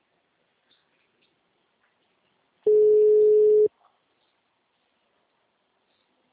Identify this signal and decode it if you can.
Ringback tone (Europe, 425 Hz).